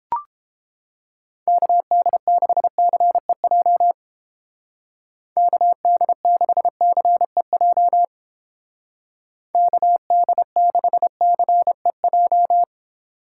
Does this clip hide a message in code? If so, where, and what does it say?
Yes: Morse code from 1.5 to 12.6 s, heard as a keyed tone, about 30 wpm, 700 Hz. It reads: KD6CEJ KD6CEJ KD6CEJ